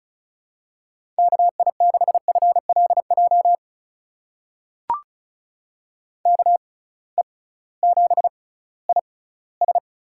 Morse code, heard as a keyed tone, about 36 wpm, spaced as for 19 wpm, 700 Hz. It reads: KI6FLJ KE7IS
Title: USA Call Signs - 6x Character Spacing - 35wpm